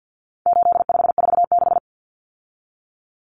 Morse code, heard as a keyed tone, about 50 wpm, 700 Hz. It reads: 8546
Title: Common Cut Numbers - 4-Digit 50wpm